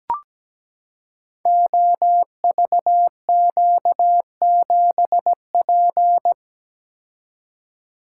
Morse code, heard as a keyed tone, about 17 wpm, 700 Hz. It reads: OVQ7P
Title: Code Groups - 5 Letter:Numbers - 17wpm